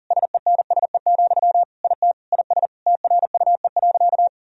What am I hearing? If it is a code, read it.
Morse code, heard as a keyed tone, about 40 wpm, 700 Hz. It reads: SENSE, IT IS TRUE.
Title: Sentences from Top 1000 Words - Review Part 4 - 40wpm